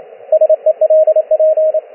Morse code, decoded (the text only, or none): SELP